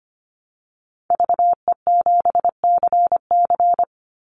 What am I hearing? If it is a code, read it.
Morse code, heard as a keyed tone, about 25 wpm, 700 Hz. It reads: VE7CC
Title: Call Signs - 25wpm